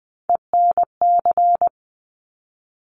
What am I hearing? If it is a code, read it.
Morse code, heard as a keyed tone, about 20 wpm, 700 Hz. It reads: ENC
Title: Top Three Letter Combinations in Words 20wpm